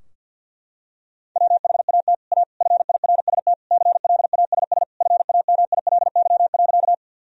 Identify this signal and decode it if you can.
Morse code, heard as a keyed tone, about 52 wpm, 700 Hz. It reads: WHAT A FIRST CLASS FAMILY.
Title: Sentences from Top 1000 Words - Review Part 2 - 50wpm